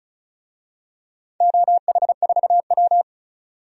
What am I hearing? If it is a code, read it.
Morse code, heard as a keyed tone, about 36 wpm, 700 Hz. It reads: OH4W
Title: DXCC-100 Call Signs - 4 Characters - 35wpm